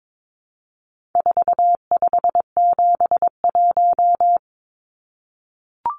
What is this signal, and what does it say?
Morse code, heard as a keyed tone, about 22 wpm, 700 Hz. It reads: 4571
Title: Numbers - 4-Digit 22wpm